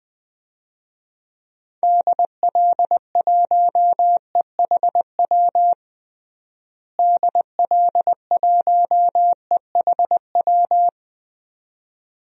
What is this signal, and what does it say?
Morse code, heard as a keyed tone, about 20 wpm, 690 Hz. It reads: DL1EHW DL1EHW
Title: DXCC-100 Call Signs - 6 Characters - Repeated 2x - 20wpm